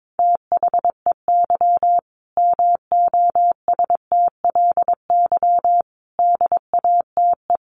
Morse code, heard as a keyed tone, about 22 wpm, 700 Hz. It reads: THEY MOSTLY DATE
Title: Sentences from Top 1300 Words 22wpm